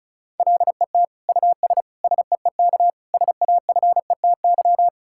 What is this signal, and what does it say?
Morse code, heard as a keyed tone, about 36 wpm, 700 Hz. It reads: LET US SEEK SAFETY
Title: Sentences from Top 1600 Words 35wpm